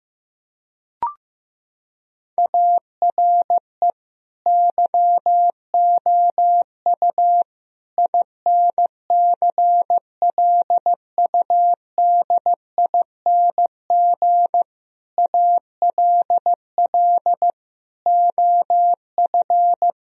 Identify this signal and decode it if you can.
Morse code, heard as a keyed tone, about 15 wpm, 700 Hz. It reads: ARE YOU INCLUDING ALL OF
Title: Sentences from Top 1000 Words - Review Part 4 - 15wpm